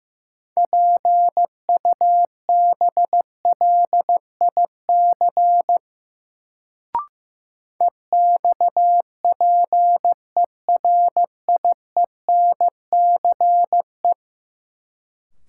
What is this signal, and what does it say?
Morse code, heard as a keyed tone, about 15 wpm, 700 Hz. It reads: PUBLIC EXPERIENCE
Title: Sentences from Top 600 Words - New Word Drill - 15wpm